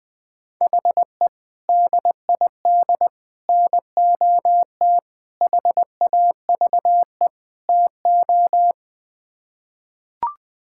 Morse code, heard as a keyed tone, about 21 wpm, 700 Hz. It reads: HE DID NOT HAVE TO
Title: Build A Phrase - 5-Word Phrases - 20wpm